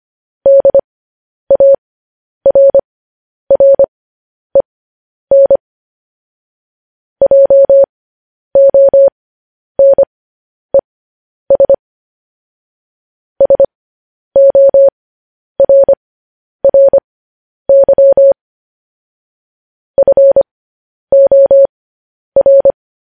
Morse code, heard as a keyed tone, about 25 wpm, spaced as for 10 wpm, 550 Hz. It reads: DARREN JONES SORRY FOR
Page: news-headlines-in-morse-code-at-10-wpm